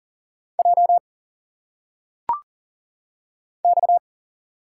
Morse code, heard as a keyed tone, about 38 wpm, 700 Hz. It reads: J X
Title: Single Letters - Phonetic Pronunciation 40wpm